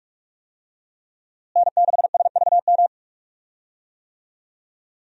Morse code, heard as a keyed tone, about 46 wpm, 700 Hz. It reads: N6SVM
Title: USA Call Signs - 5 Characters - Repeated 3x - 45wpm